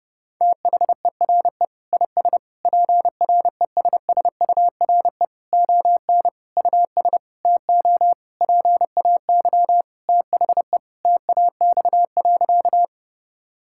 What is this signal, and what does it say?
Morse code, heard as a keyed tone, about 32 wpm, 700 Hz. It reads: THERE IS PRESSURE ON US TO PAY THE TAX.
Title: Sentences from Top 1000 Words - Review Part 6 - 30wpm